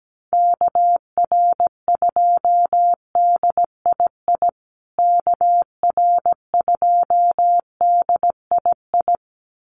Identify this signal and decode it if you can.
Morse code, heard as a keyed tone, about 17 wpm, 700 Hz. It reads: KR2DII KR2DII
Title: USA Call Signs Repeated 2x 17wpm